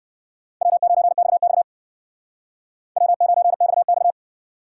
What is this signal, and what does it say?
Morse code, heard as a keyed tone, about 57 wpm, 700 Hz. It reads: W9XX W9XX